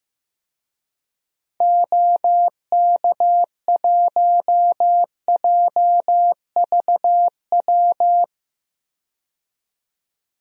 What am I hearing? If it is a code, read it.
Morse code, heard as a keyed tone, about 15 wpm, 700 Hz. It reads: OK1JVW
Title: DXCC-100 Call Signs - 6 Characters - 15wpm